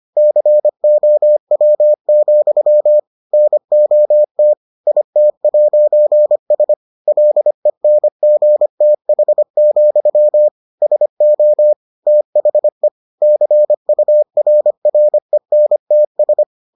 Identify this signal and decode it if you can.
Morse code, heard as a keyed tone, about 25 wpm, 600 Hz. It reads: COW, NOT IT'S LENGTH, SO THE CURRENTS